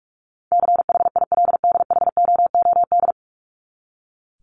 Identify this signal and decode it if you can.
Morse code, heard as a keyed tone, about 44 wpm, 700 Hz. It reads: CHILDHOOD
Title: Top 1751-2000 Words 45wpm